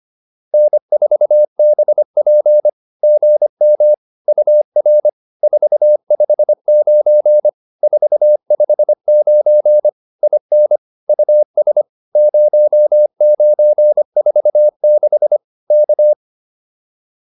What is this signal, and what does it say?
Morse code, heard as a keyed tone, about 25 wpm, 600 Hz. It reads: N4BP GM UR 459 459 IN US 0946 K